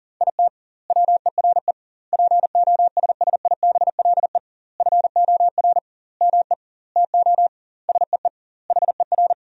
Morse code, heard as a keyed tone, about 38 wpm, 700 Hz. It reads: IT WERE POSSIBLE FOR ME TO SEE HER
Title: 2x Word-Spacing - Sentences from Top 500 Words - 40wpm